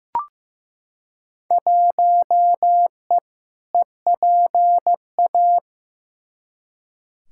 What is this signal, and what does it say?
Morse code, heard as a keyed tone, about 15 wpm, 700 Hz. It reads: 1E EPA
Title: Field Day Exchanges - 15wpm